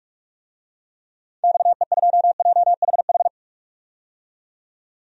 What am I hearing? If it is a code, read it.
Morse code, heard as a keyed tone, about 45 wpm, 700 Hz. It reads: XE2JHH
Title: DXCC-100 Call Signs - 6 Characters - 45wpm